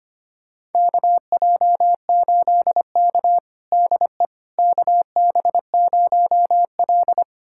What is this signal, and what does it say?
Morse code, heard as a keyed tone, about 25 wpm, 700 Hz. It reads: KJ8K DE KB0L
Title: USA Callsign DE Callsign - 4 Characters - 25wpm